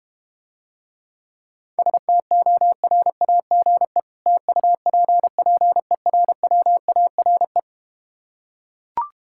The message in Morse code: STORAGE TUPPERWARE

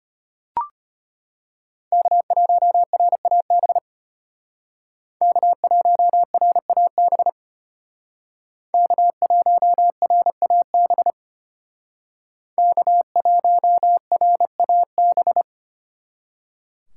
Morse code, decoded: K1RAB K1RAB K1RAB K1RAB

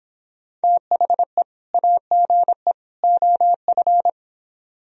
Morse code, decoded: THE AGE OF

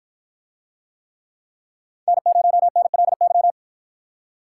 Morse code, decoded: N0NLY

